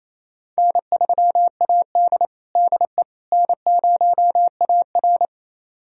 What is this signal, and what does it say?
Morse code, heard as a keyed tone, about 27 wpm, 700 Hz. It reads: N3AD DE N0AR
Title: USA Callsign DE Callsign - 4 Characters - 28wpm